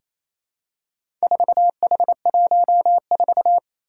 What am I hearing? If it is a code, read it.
Morse code, heard as a keyed tone, about 28 wpm, 700 Hz. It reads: 4H14
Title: Four Challenging Letter-Number Combinations 28wpm